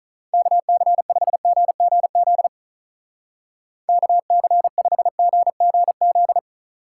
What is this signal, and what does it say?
Morse code, heard as a keyed tone, about 44 wpm, 700 Hz. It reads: KC5GGZ KC5GGZ
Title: Speed Racing - USA 2x3 Call Signs 35wpm